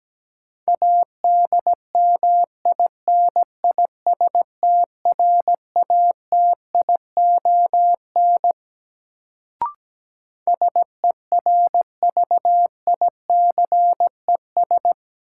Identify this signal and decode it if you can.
Morse code, heard as a keyed tone, about 17 wpm, 700 Hz. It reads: ADMINISTRATION SERVICES
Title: Sets of 5 Words - New Word Drill - 17wpm